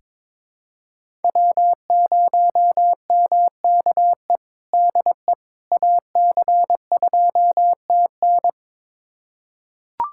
Morse code, heard as a keyed tone, about 22 wpm, 700 Hz. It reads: W0MKE DE AC2TN
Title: USA Callsign DE Callsign - 5 Characters - 22wpm